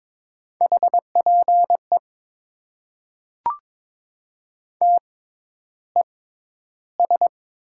Morse code, heard as a keyed tone, about 22 wpm, 700 Hz. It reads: H P E T E S
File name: Top 90 QSO Elements - 6x Character Spacing - 22wpm